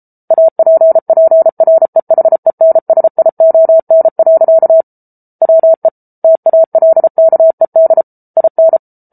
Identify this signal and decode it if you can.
Morse code, heard as a keyed tone, about 34 wpm, 650 Hz. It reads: APPREHENSION. WE TALKED IN